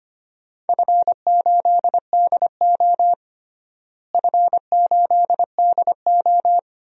Morse code, heard as a keyed tone, about 25 wpm, 700 Hz. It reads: F8DO F8DO
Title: DXCC-100 Call Signs - 4 Characters - Repeated 3x - 25wpm